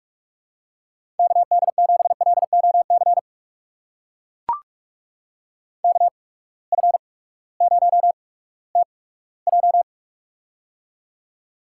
Morse code, heard as a keyed tone, about 44 wpm, spaced as for 20 wpm, 700 Hz. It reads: KD7LOC KF0TJ
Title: USA Call Signs - 8x Character Spacing - 45wpm